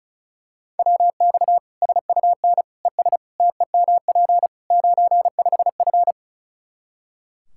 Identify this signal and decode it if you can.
Morse code, heard as a keyed tone, about 36 wpm, 700 Hz. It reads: WX SUN ES TEMP 95F
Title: K1USN SST QSO Practice 35wpm